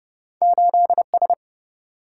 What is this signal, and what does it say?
Morse code, heard as a keyed tone, about 30 wpm, 700 Hz. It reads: 8S
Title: Single-Letter Single-Number 30wpm